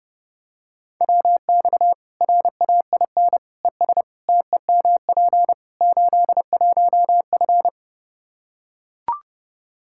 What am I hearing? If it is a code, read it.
Morse code, heard as a keyed tone, about 30 wpm, 700 Hz. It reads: WX RAIN ES TEMP 81F
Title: Practice QSO - Exchange 2 - Info - 30wpm